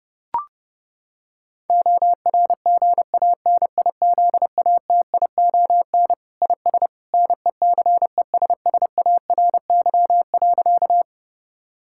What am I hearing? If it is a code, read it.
Morse code, heard as a keyed tone, about 30 wpm, 700 Hz. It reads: ORGANIZATION IS NECESSARY.
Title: Sentences from Top 1000 Words - Review Part 5 - 30wpm